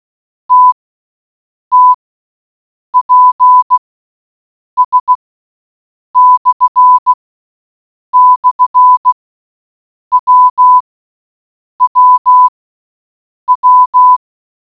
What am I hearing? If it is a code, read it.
Morse code, heard as a keyed tone, about 16 wpm, spaced as for 7 wpm, 1000 Hz. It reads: TTPS//WWW